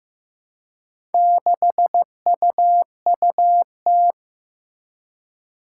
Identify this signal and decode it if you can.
Morse code, heard as a keyed tone, about 15 wpm, 700 Hz. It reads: 6UUT
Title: Four Challenging Letter-Number Combinations 15wpm